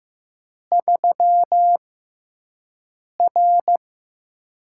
Morse code, heard as a keyed tone, about 15 wpm, spaced as for 5 wpm, 700 Hz. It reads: 3R
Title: USA Call Signs - 6x Character Spacing - 15wpm